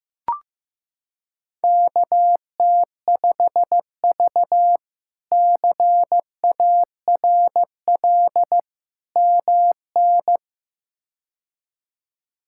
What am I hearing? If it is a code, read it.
Morse code, heard as a keyed tone, about 15 wpm, 700 Hz. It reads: KT5V CARL MN
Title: K1USN SST QSO Practice - 15wpm